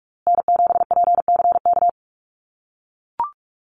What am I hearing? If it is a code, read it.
Morse code, heard as a keyed tone, about 44 wpm, 700 Hz. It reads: N7PCK
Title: USA Call Signs Repeated 2x 45wpm